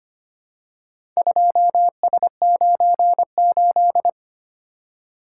Morse code, heard as a keyed tone, about 25 wpm, 700 Hz. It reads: 2S98
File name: practice / 25wpm / Four Letter-Number Combinations - 25wpm